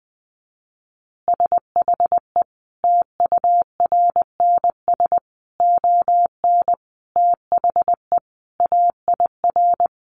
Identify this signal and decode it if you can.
Morse code, heard as a keyed tone, about 20 wpm, 700 Hz. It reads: SHE TURNS ON THE AIR
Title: Easy Phrases 20wpm